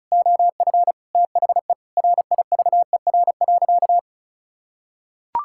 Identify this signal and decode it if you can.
Morse code, heard as a keyed tone, about 36 wpm, 700 Hz. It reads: OF THE RIVER.
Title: Sentences from Top 1000 Words - Review Part 6 - 35wpm